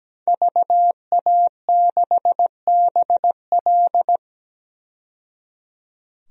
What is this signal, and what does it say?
Morse code, heard as a keyed tone, about 17 wpm, 700 Hz. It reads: VA6BL
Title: DXCC-100 Call Signs - 5 Characters - 17wpm